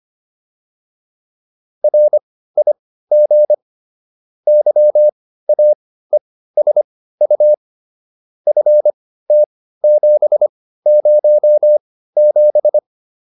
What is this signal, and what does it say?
Morse code, heard as a keyed tone, about 25 wpm, spaced as for 15 wpm, 600 Hz. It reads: RIG YAESU FT707